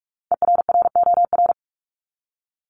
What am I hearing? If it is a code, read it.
Morse code, heard as a keyed tone, about 44 wpm, 700 Hz. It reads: ERROR
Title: Top 1751-2000 Words 45wpm